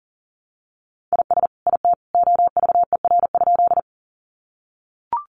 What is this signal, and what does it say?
Morse code, heard as a keyed tone, about 40 wpm, 700 Hz. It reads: IS IT OVER?
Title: Sentences from Top 100 Words 40wpm